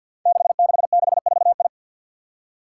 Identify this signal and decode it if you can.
Morse code, heard as a keyed tone, about 53 wpm, 700 Hz. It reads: 6664I
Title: Five Challenging Letter-Number Combinations 50wpm